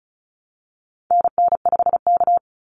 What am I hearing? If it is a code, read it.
Morse code, heard as a keyed tone, about 36 wpm, 700 Hz. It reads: NN5K
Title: NA SOTA Activator Callsigns 35wpm